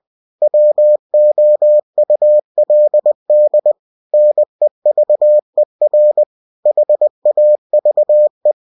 Morse code, heard as a keyed tone, about 20 wpm, 600 Hz. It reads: WOULD NEVER HAVE